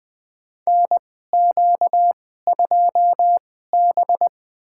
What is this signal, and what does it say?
Morse code, heard as a keyed tone, about 20 wpm, spaced as for 14 wpm, 700 Hz. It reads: NQ2B
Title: USA Call Signs - 4 Characters - 2x Character Spacing - 20wpm